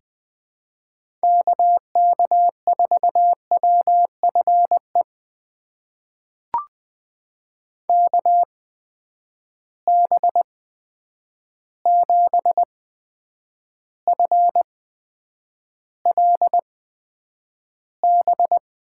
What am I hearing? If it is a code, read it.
Morse code, heard as a keyed tone, about 21 wpm, spaced as for 9 wpm, 700 Hz. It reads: KK4WFE KB7FLB
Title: USA Call Signs - 6 Characters - 8x Character Spacing - 20wpm